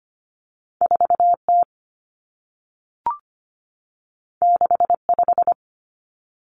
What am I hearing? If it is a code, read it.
Morse code, heard as a keyed tone, about 25 wpm, 700 Hz. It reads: 4T 65
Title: Common Cut Numbers - 2-Digit 25wpm